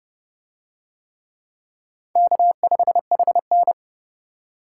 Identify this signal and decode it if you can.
Morse code, heard as a keyed tone, about 31 wpm, 700 Hz. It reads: K5HN